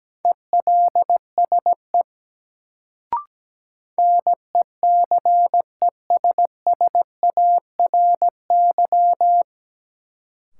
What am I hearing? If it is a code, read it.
Morse code, heard as a keyed tone, about 17 wpm, 700 Hz. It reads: ELSE NECESSARY
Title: Sentences from Top 800 Words - New Word Drill - 17wpm